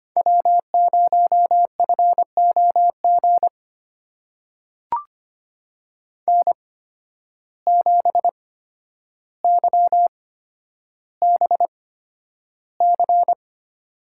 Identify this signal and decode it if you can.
Morse code, heard as a keyed tone, about 25 wpm, spaced as for 11 wpm, 700 Hz. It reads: W0FOG N7YBC
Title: USA Call Signs - 5 Characters - 8x Character Spacing - 25wpm